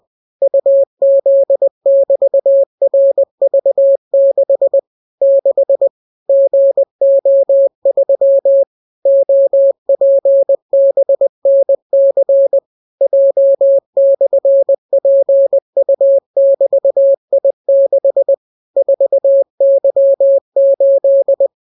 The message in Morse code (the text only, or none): UZ<BT>RV6 6 GO3 OPBNC J/PU<BT>I6 4Y8